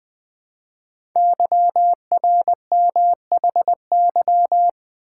Morse code, heard as a keyed tone, about 20 wpm, 700 Hz. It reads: YRMHY
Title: Code Groups - 5 Letters - 20wpm